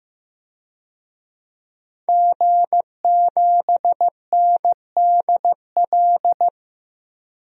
Morse code, heard as a keyed tone, about 15 wpm, 700 Hz. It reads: G7NDL
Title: DXCC-100 Call Signs - 5 Characters - Repeated 3x - 15wpm